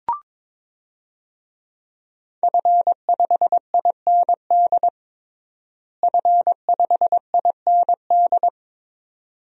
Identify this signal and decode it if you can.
Morse code, heard as a keyed tone, about 22 wpm, 700 Hz. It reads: F5IND F5IND
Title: DXCC-100 Call Signs - 5 Characters - Repeated 2x - 22wpm